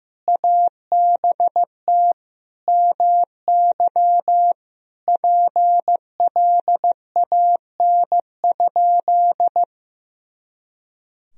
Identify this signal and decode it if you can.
Morse code, heard as a keyed tone, about 15 wpm, 700 Hz. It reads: ABT MY PLAN?
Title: Sentences from Top 500 Words and Abbreviations 15wpm